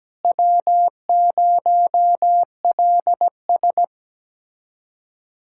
Morse code, heard as a keyed tone, about 17 wpm, 700 Hz. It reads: W0LS
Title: USA Callsign DE Callsign - 4 Characters - 17wpm